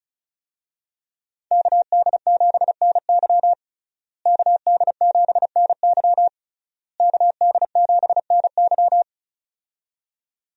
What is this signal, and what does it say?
Morse code, heard as a keyed tone, about 34 wpm, 700 Hz. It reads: KD7NY KD7NY KD7NY
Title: USA Call Signs - 5 Characters - Repeated 3x - 35wpm